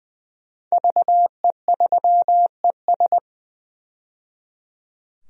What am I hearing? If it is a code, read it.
Morse code, heard as a keyed tone, about 20 wpm, 700 Hz. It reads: VE3ES
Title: DXCC-100 Call Signs - 5 Characters - 20wpm